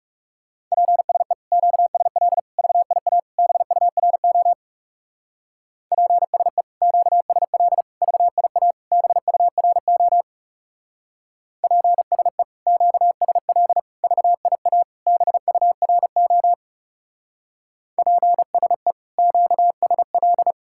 Morse code, heard as a keyed tone, about 36 wpm, 700 Hz. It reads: PSE QSL VIA BURO PSE QSL VIA BURO PSE QSL VIA BURO PSE QSL
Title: Speed-Racing Common QSO Phrases - 30wpm